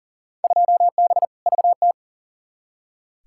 Morse code, heard as a keyed tone, about 40 wpm, 700 Hz. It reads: 2B VT
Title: Field Day Exchanges Encore - 40wpm